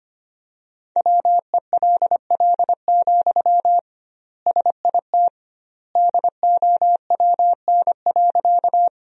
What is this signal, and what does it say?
Morse code, heard as a keyed tone, about 25 wpm, 700 Hz. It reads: WELL, SIT DOWN.
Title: 2x Word-Spacing - Sentences from Top 500 Words - 25wpm